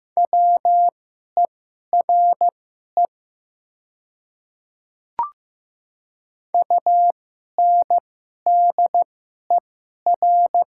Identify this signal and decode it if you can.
Morse code, heard as a keyed tone, about 15 wpm, spaced as for 11 wpm, 690 Hz. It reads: WERE UNDER
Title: 2x Character Spacing - Top 200 Words - 15wpm